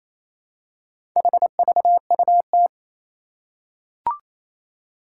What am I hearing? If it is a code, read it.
Morse code, heard as a keyed tone, about 28 wpm, 700 Hz. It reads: HVUT
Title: Four Challenging Letter-Number Combinations 28wpm